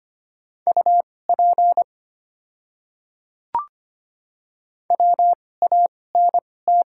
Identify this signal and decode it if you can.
Morse code, heard as a keyed tone, about 25 wpm, spaced as for 18 wpm, 700 Hz. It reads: UP WANT
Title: 2x Character Spacing - Top 200 Words - 25wpm